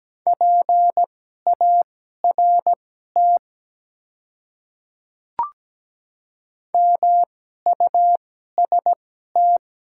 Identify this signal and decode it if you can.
Morse code, heard as a keyed tone, about 17 wpm, spaced as for 12 wpm, 700 Hz. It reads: PART MUST
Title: 2x Character Spacing - Top 300 Words - 17wpm